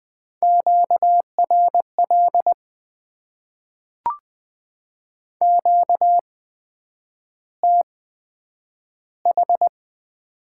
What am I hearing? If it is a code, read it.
Morse code, heard as a keyed tone, about 20 wpm, spaced as for 9 wpm, 700 Hz. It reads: QRL QTH
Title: Top 20 QSO Elements - 8x Character Spacing - 20wpm